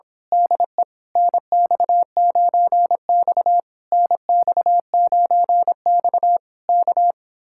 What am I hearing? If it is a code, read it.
Morse code, heard as a keyed tone, about 26 wpm, 700 Hz. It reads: DE NX9X NX9X K
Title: Speed-Racing Common QSO Phrases - 22wpm